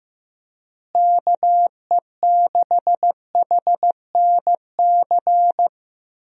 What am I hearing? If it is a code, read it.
Morse code, heard as a keyed tone, about 15 wpm, 700 Hz. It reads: KE6HNC
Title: USA Call Signs - 6 Characters - Repeated 2x - 15wpm